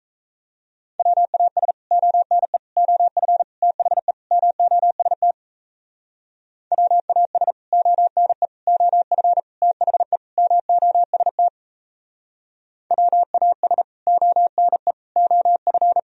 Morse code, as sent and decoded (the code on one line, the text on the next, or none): .-- .- ... / --- -. / --- ..-. / - .... / -- --- ... - / .-- .- ... / --- -. / --- ..-. / - .... / -- --- ... - / .-- .- ... / --- -. / --- ..-.
WAS ONE OF THE MOST WAS ONE OF THE MOST WAS ONE OF